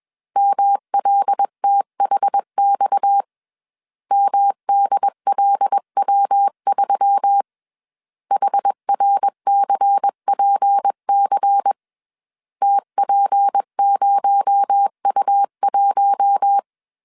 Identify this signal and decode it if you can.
Morse code, heard as a keyed tone, about 21 wpm, 800 Hz. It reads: MLTHX MDLW3 HRCPC TP0U1
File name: CW / DET / 1 6 1500C S B 20wpm 2048